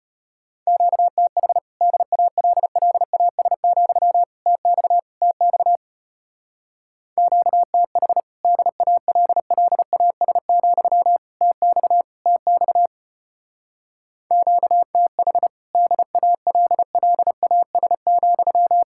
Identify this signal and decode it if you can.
Morse code, heard as a keyed tone, about 33 wpm, 700 Hz. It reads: QTH DALLAS, TX TX QTH DALLAS, TX TX QTH DALLAS,